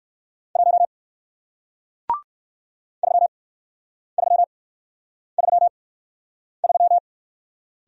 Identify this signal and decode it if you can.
Morse code, heard as a keyed tone, about 59 wpm, spaced as for 9 wpm, 700 Hz. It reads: 1 3333